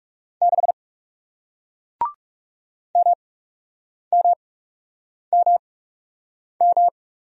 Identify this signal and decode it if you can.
Morse code, heard as a keyed tone, about 39 wpm, spaced as for 8 wpm, 700 Hz. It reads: 6 MMMM